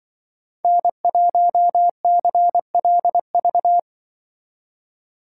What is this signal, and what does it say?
Morse code, heard as a keyed tone, about 24 wpm, 700 Hz. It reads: N1CLV
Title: Speed Racing - USA 1x3 Call Signs 20wpm